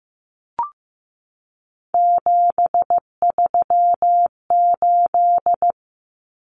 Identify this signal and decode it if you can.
Morse code, heard as a keyed tone, about 15 wpm, 700 Hz. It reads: 738
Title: Numbers - 3-Digit 15wpm